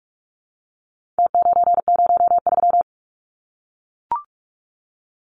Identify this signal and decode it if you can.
Morse code, heard as a keyed tone, about 44 wpm, 700 Hz. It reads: T903
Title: Numbers - Contest Style - 4-Digit 45wpm